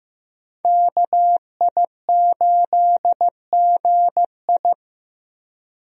Morse code, heard as a keyed tone, about 15 wpm, 700 Hz. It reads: KI8GI
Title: USA Call Signs - 5 Characters - Repeated 2x - 15wpm